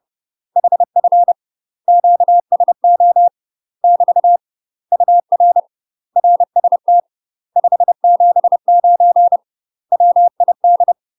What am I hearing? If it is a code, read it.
Morse code, heard as a keyed tone, about 30 wpm, 700 Hz. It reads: HF QSO <BT> UR RST 579 WID
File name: UK MASTER QSOs TRAINING / 30WPM / UK MASTER QSOs 0004